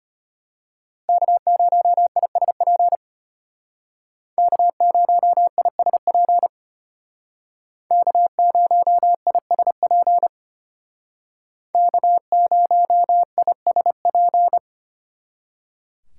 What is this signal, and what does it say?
Morse code, heard as a keyed tone, about 30 wpm, 700 Hz. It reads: K0ISP K0ISP K0ISP K0ISP
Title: Speed Racing - USA 1x3 Call Signs 25wpm